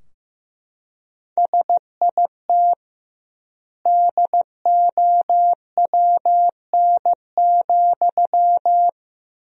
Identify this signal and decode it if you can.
Morse code, heard as a keyed tone, about 16 wpm, 700 Hz. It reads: SIT DOWN,
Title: 2x Word-Spacing - Sentences from Top 500 Words - 15wpm